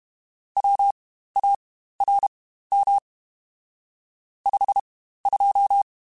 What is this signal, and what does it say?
Morse code, heard as a keyed tone, about 32 wpm, spaced as for 15 wpm, 800 Hz. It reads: WARM 52
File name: Spaced Characters / Spaced Characters-mp3s / Weather4 10 sp @30